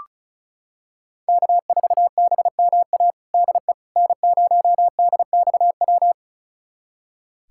Morse code, heard as a keyed tone, about 36 wpm, 700 Hz. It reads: K4BMA DE N0DXW
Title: USA Callsign DE Callsign - 5 Characters - 35wpm